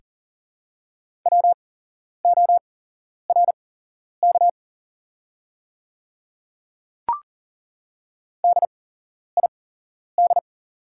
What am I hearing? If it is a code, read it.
Morse code, heard as a keyed tone, about 40 wpm, spaced as for 11 wpm, 690 Hz. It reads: WORK DID
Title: 8x Character Spacing - Top 200 Words - 40wpm